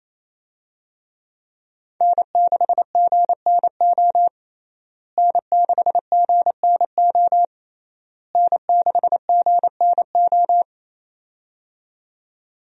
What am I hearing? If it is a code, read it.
Morse code, heard as a keyed tone, about 28 wpm, 700 Hz. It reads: N6GNO N6GNO N6GNO
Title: USA Call Signs - 5 Characters - Repeated 3x - 28wpm